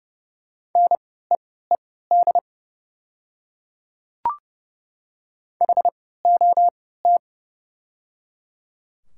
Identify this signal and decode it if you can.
Morse code, heard as a keyed tone, about 29 wpm, spaced as for 17 wpm, 690 Hz. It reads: NEED HOT
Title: 3x Character Spacing - Top 200 Words - 30wpm